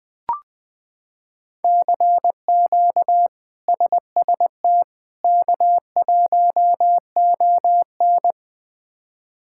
Morse code, heard as a keyed tone, about 20 wpm, 700 Hz. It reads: CQ SST K1ON
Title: K1USN SST QSO Practice - 20wpm